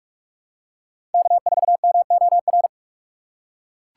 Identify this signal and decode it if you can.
Morse code, heard as a keyed tone, about 40 wpm, 700 Hz. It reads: K4MOR